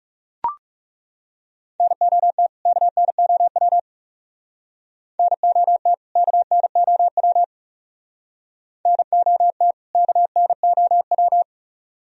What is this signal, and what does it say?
Morse code, heard as a keyed tone, about 37 wpm, 700 Hz. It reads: NOT KNOW NOT KNOW NOT KNOW
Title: Speed-Racing - Sets of 2 Words 30wpm